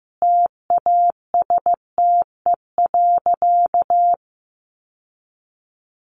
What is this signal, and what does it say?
Morse code, heard as a keyed tone, about 15 wpm, 700 Hz. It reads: TASTE.
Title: Sentences from Top 1100 Words 15wpm